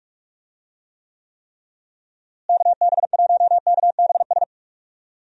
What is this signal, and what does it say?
Morse code, heard as a keyed tone, about 42 wpm, 700 Hz. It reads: KB1KBS
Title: USA Call Signs - 6 Characters - Repeated 3x - 45wpm